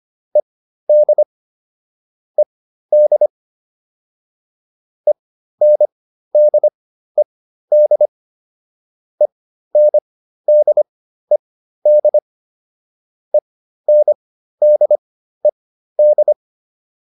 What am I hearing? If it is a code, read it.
Morse code, heard as a keyed tone, about 25 wpm, spaced as for 13 wpm, 600 Hz. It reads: ED ED ENDED ENDED ENDED